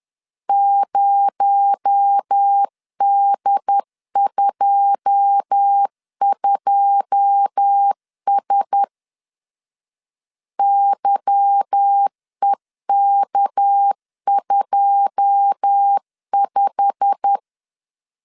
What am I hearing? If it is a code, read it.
Morse code, heard as a keyed tone, about 10 wpm, 800 Hz. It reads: 0D22S YEK25
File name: LEZIONI / 9 2 10wpm 2736